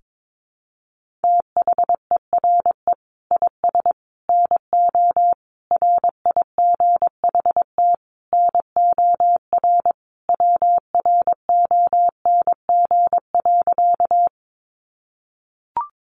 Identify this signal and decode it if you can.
Morse code, heard as a keyed tone, about 22 wpm, 700 Hz. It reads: THERE IS NO RIGHT NOR WRONG.
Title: Sentences from Top 1500 Words 22wpm